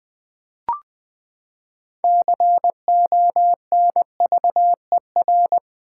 Morse code, heard as a keyed tone, about 20 wpm, 700 Hz. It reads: CONVER
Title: Build A Word - 6 Letters At A Time - 12-Letter Words - 20wpm